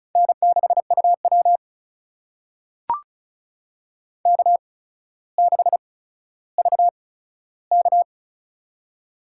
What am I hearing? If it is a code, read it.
Morse code, heard as a keyed tone, about 35 wpm, spaced as for 15 wpm, 700 Hz. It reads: N6UW K6VK
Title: USA Call Signs - 4 Characters - 8x Character Spacing - 35wpm